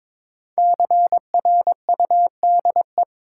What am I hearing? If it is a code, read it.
Morse code, heard as a keyed tone, about 22 wpm, 690 Hz. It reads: CRUDE
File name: 5 Letter Words - First 3 Letters Given - 22wpm